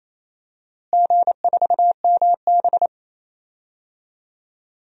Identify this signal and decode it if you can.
Morse code, heard as a keyed tone, about 28 wpm, 700 Hz. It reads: G4MB
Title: DXCC-100 Call Signs - 4 Characters - Repeated 3x - 28wpm